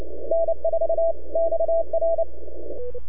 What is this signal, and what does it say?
Morse code, heard as a keyed tone, about 29 wpm, 660 Hz. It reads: N4XR
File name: NAQPCW-8-2009 / W1UJ NAQP CW 09 1930Z